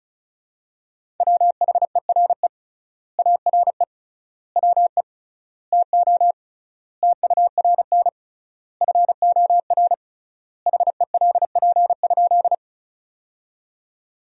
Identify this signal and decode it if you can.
Morse code, heard as a keyed tone, about 36 wpm, 700 Hz. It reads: WHERE ARE WE TO TURN FOR HELP?
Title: 3x Word-Spacing - Sentences from Top 200 Words - 35wpm